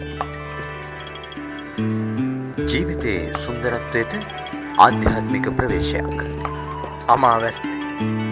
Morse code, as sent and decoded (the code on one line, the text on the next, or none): none